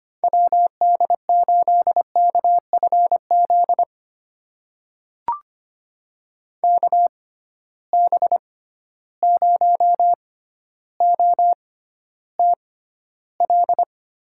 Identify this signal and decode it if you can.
Morse code, heard as a keyed tone, about 25 wpm, spaced as for 13 wpm, 700 Hz. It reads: WD8KFZ KB0OTL
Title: USA Call Signs - 6 Characters - 6x Character Spacing - 25wpm